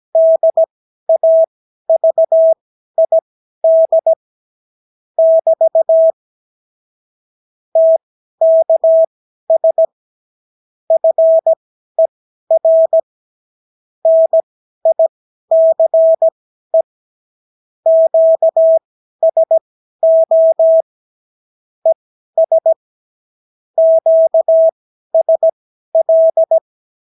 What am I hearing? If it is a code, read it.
Morse code, heard as a keyed tone, about 17 wpm, spaced as for 12 wpm, 650 Hz. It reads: DAVID <BT> TKS FER NICE QSO ES QSL
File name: CW QSO libroCW4 18-8ppm